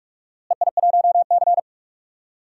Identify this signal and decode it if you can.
Morse code, heard as a keyed tone, about 47 wpm, 700 Hz. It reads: EI1C
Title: DXCC-100 Call Signs - 4 Characters - Repeated 3x - 45wpm